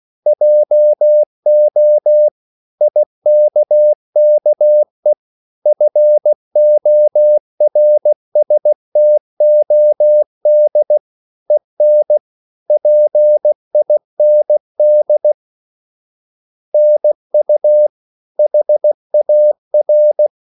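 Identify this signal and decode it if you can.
Morse code, heard as a keyed tone, about 16 wpm, 600 Hz. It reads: JO IKKE FORSTOD EN PIND NU HAR
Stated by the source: Slow (QRS) Morse code at 16 wpm, from CW med Gnister.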